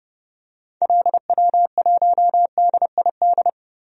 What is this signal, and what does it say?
Morse code, heard as a keyed tone, about 32 wpm, 700 Hz. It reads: LW1DID
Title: DXCC-100 Call Signs - 6 Characters - 30wpm